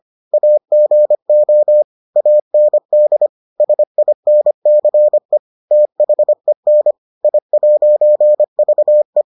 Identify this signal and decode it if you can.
Morse code, heard as a keyed tone, about 25 wpm, 600 Hz. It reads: AGO AND SINCE THEN I'VE